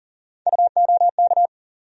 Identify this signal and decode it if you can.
Morse code, heard as a keyed tone, about 40 wpm, 700 Hz. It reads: UOK
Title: USA Call Sign Postfixes - 3 Letters 40wpm